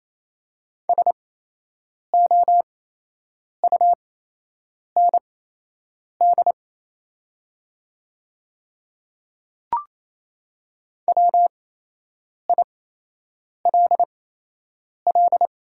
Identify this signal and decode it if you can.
Morse code, heard as a keyed tone, about 28 wpm, spaced as for 8 wpm, 700 Hz. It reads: SOUND WILL
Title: 8x Character Spacing - Top 100 Words - 28wpm